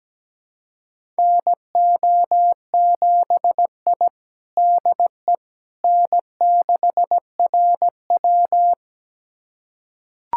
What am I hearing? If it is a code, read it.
Morse code, heard as a keyed tone, about 17 wpm, 700 Hz. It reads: NO7I DE N6RW
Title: USA Callsign DE Callsign - 4 Characters - 17wpm